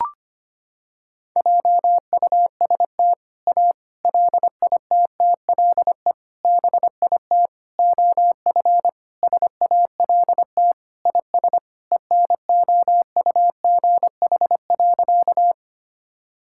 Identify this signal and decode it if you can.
Morse code, heard as a keyed tone, about 25 wpm, 700 Hz. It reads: JUST A LITTLE BIT OF SALT IS ENOUGH.
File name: practice / 25wpm / Sentences with 4-Word Phrases - 25wpm